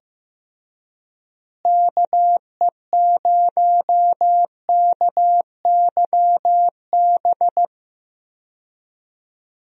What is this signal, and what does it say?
Morse code, heard as a keyed tone, about 15 wpm, 700 Hz. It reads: KE0KYB
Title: USA Call Signs - 6 Characters - Repeated 3x - 15wpm